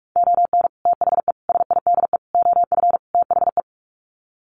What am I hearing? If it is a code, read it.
Morse code, heard as a keyed tone, about 44 wpm, 700 Hz. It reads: ON THE SIDE OF THE
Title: Sets of 5 Words 45wpm